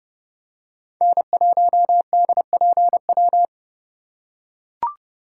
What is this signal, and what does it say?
Morse code, heard as a keyed tone, about 32 wpm, 690 Hz. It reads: N1DPW
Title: USA Call Signs - 4x Character Spacing - 30wpm